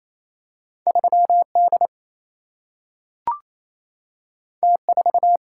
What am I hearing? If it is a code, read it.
Morse code, heard as a keyed tone, about 28 wpm, 700 Hz. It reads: 3 D T 4
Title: Single-Letter Single-Number 28wpm